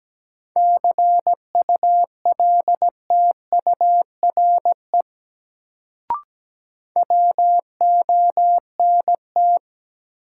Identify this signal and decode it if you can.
Morse code, heard as a keyed tone, about 17 wpm, 700 Hz. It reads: CULTURE WONT
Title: Sentences from Top 1000 Words - Review Part 6 - New Word Drill - 17wpm